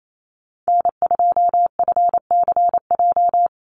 Morse code, heard as a keyed tone, about 28 wpm, 700 Hz. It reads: N2FCJ
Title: USA Call Signs 28wpm